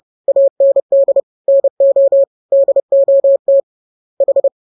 Morse code, heard as a keyed tone, about 30 wpm, 550 Hz. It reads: AND NO DOT H
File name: jokes / joke 20 30 0000